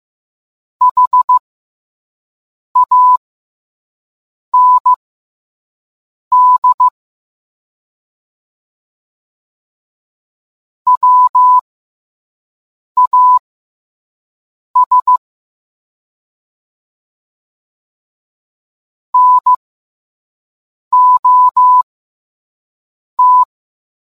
Morse code, heard as a keyed tone, about 15 wpm, spaced as for 5 wpm, 1000 Hz. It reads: HAND WAS NOT